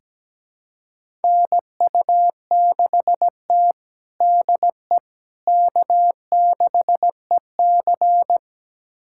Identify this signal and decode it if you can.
Morse code, heard as a keyed tone, about 17 wpm, 700 Hz. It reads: NU6T DE K6EC
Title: USA Callsign DE Callsign - 4 Characters - 17wpm